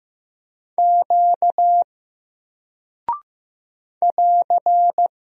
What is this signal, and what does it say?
Morse code, heard as a keyed tone, about 15 wpm, 700 Hz. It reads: Q <AR>
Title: Single Character (Letter:Number:Punctuation:Prosign) Alt 15wpm